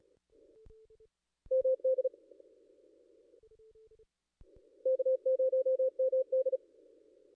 Morse code, heard as a keyed tone, about 37 wpm, 520 Hz. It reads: MD K0MD
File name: contests / ss-cw-k5tr-op-n5rz -1600